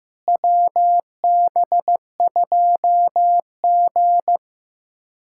Morse code, heard as a keyed tone, about 15 wpm, 700 Hz. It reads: WB2G
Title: USA Callsign DE Callsign - 4 Characters - 15wpm